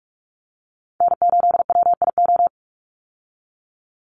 Morse code, heard as a keyed tone, about 44 wpm, 700 Hz. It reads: N8WIO
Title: USA Call Signs 45wpm